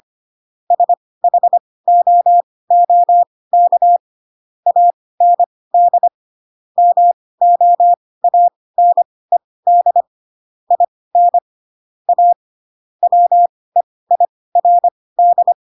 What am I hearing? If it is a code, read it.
Morse code, heard as a keyed tone, about 25 wpm, spaced as for 18 wpm, 700 Hz. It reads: SHOOK AND MOANED IN A WEIRD
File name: Pinocchio / Pinocchio ch6 12 25 ik1hge